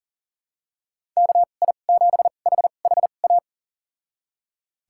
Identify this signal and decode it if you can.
Morse code, heard as a keyed tone, about 40 wpm, spaced as for 29 wpm, 700 Hz. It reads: KI7HHA
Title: USA Call Signs - 2x Character Spacing - 40wpm